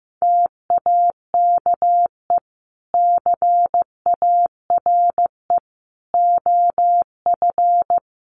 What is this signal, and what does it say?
Morse code, heard as a keyed tone, about 15 wpm, 700 Hz. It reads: TAKE CARE OF
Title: Sets of 3 Words 15wpm